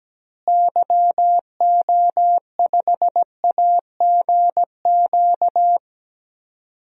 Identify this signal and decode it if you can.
Morse code, heard as a keyed tone, about 17 wpm, 700 Hz. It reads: YO5AGQ
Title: DXCC-100 Call Signs - 6 Characters - Repeated 2x - 17wpm